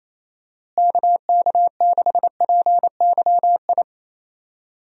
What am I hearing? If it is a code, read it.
Morse code, heard as a keyed tone, about 27 wpm, 700 Hz. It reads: KK6PYI
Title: USA Call Signs - 6 Characters - Repeated 3x - 28wpm